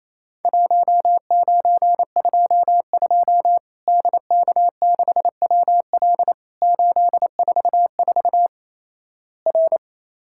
Morse code, heard as a keyed tone, about 28 wpm, 700 Hz. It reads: 1922 DK6WL 844 R
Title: Worked All Europe DX Contest - QTC Exchange - 28wpm